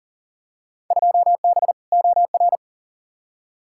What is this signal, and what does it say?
Morse code, heard as a keyed tone, about 40 wpm, 700 Hz. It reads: 2B OR
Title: Field Day Exchanges Encore - 40wpm